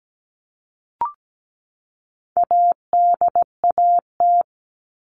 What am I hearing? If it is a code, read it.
Morse code, heard as a keyed tone, about 17 wpm, 700 Hz. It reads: ADAT
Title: All Cut Numbers - 4-Digit 17wpm